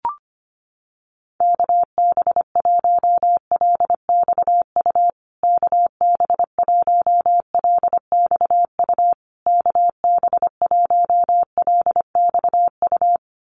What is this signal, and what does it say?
Morse code, heard as a keyed tone, about 25 wpm, 700 Hz. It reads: KB1LXU KB1LXU KB1LXU
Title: USA Call Signs Repeated 3x 25wpm